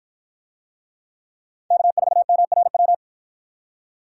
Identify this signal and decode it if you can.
Morse code, heard as a keyed tone, about 50 wpm, 700 Hz. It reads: K4MRW